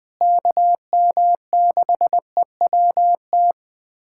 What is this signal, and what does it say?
Morse code, heard as a keyed tone, about 20 wpm, 700 Hz. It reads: KM6EWT
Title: USA Callsign DE Callsign - 6 Characters - 20wpm